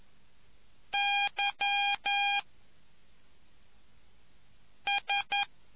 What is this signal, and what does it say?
Morse code, heard as a keyed tone, about 11 wpm, 2400 Hz. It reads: Y S